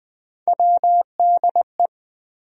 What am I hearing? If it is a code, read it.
Morse code, heard as a keyed tone, about 20 wpm, 700 Hz. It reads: WDE
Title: USA Call Sign Postfixes - 3 Letters 20wpm